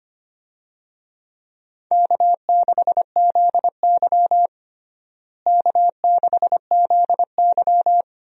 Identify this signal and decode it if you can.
Morse code, heard as a keyed tone, about 25 wpm, 700 Hz. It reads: K6ZY K6ZY
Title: USA Call Signs - 4 Characters - Repeated 2x - 25wpm